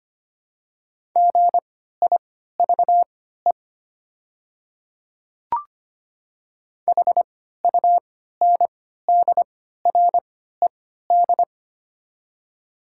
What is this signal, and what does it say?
Morse code, heard as a keyed tone, about 25 wpm, spaced as for 14 wpm, 690 Hz. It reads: GIVE HUNDRED
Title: 3x Character Spacing - Top 500 Words - 25wpm